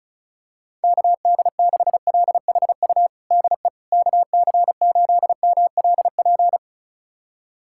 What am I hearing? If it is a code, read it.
Morse code, heard as a keyed tone, about 36 wpm, 700 Hz. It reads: KD6LHU DE KC8MLP
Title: USA Callsign DE Callsign 35wpm